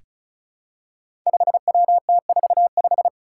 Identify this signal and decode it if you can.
Morse code, heard as a keyed tone, about 34 wpm, 700 Hz. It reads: 5WT45
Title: Five Challenging Letter-Number Combinations 35wpm